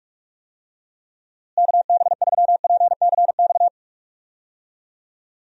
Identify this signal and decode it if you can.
Morse code, heard as a keyed tone, about 46 wpm, 700 Hz. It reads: KB3PCX
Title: USA Call Signs - 6 Characters - Repeated 3x - 45wpm